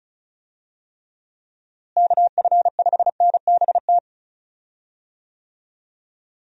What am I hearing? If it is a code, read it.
Morse code, heard as a keyed tone, about 36 wpm, 700 Hz. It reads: KF5NBT
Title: USA Call Signs - 6 Characters - Repeated 3x - 35wpm